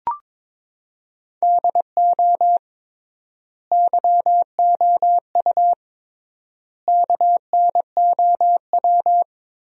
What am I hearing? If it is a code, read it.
Morse code, heard as a keyed tone, about 22 wpm, 700 Hz. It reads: DO YOU KNOW
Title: 3x Word-Spacing - Sentences from Top 100 Words - 22wpm